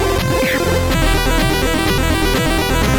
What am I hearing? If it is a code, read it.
There is no code.